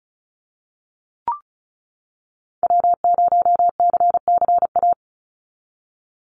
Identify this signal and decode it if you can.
Morse code, heard as a keyed tone, about 34 wpm, 700 Hz. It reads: W0CCA
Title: NA SOTA Activator Callsigns 35wpm